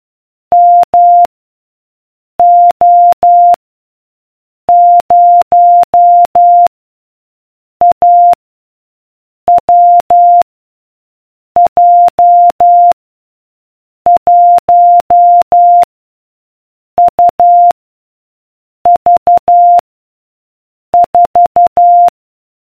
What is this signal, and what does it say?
Morse code, heard as a keyed tone, about 12 wpm, 700 Hz. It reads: M O 0 A W J 1 U V 4